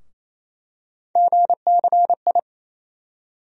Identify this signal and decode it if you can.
Morse code, heard as a keyed tone, about 28 wpm, 700 Hz. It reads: GCI
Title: USA Call Sign Postfixes - 3 Letters 28wpm